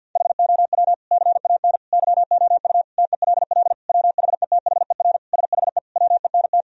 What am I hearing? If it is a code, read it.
Morse code, heard as a keyed tone, about 50 wpm, 700 Hz. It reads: HOW CAN YOU TELL WHETHER SHE WENT
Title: Sentences from Top 1000 Words - Review Part 3 - 50wpm